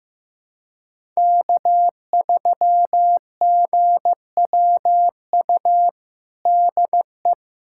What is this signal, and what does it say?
Morse code, heard as a keyed tone, about 15 wpm, 700 Hz. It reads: K3GWU DE
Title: USA Callsign DE Callsign - 5 Characters - 15wpm